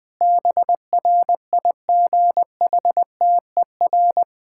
Morse code, heard as a keyed tone, about 21 wpm, 700 Hz. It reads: BRIGHTER
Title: Difficult Sentences with Hints - 20wpm